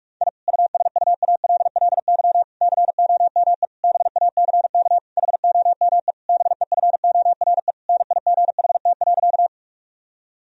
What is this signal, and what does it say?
Morse code, heard as a keyed tone, about 48 wpm, 700 Hz. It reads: I USUALLY COME BACK HOME BEFORE NIGHT.
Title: Sentences from Top 1000 Words - Review Part 3 - 45wpm